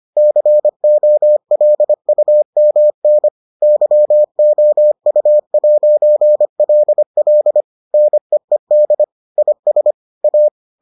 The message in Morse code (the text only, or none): COLUMN YOU'LL NEED IS A